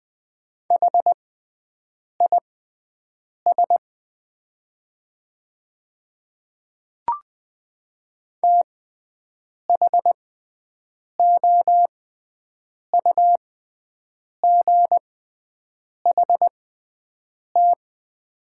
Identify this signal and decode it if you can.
Morse code, heard as a keyed tone, about 20 wpm, spaced as for 7 wpm, 700 Hz. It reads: HIS THOUGHT
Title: 6x Character Spacing - Top 300 Words - 20wpm